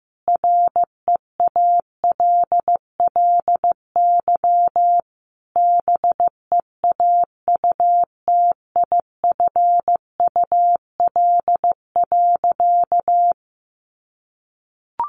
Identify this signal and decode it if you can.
Morse code, heard as a keyed tone, about 15 wpm, 700 Hz. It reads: REALLY BEAUTIFUL.
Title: Sentences from Top 1100 Words 15wpm